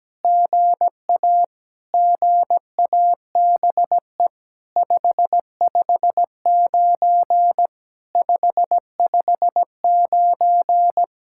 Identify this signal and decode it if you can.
Morse code, heard as a keyed tone, about 17 wpm, 700 Hz. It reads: GA GABE 559 559
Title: SOTA QSO Practice 17wpm